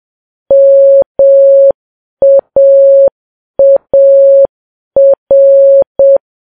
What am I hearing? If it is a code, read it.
Morse code, heard as a keyed tone, about 7 wpm, 550 Hz. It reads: MAAR